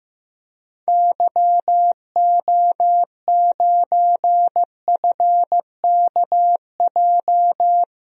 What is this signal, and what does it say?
Morse code, heard as a keyed tone, about 16 wpm, 700 Hz. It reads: YO9FKJ
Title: DXCC-100 Call Signs - 6 Characters - Repeated 2x - 15wpm